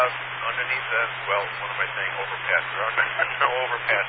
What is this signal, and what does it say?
Emergency Alert System (SAME) tones, heard as harsh buzzing bursts, with speech. It is heard from 0.5 to 3.3 s.